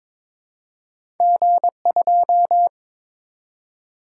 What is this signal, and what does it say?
Morse code, heard as a keyed tone, about 22 wpm, 700 Hz. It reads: G2
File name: practice / 22wpm / Two Letter-Number Combinations - 22wpm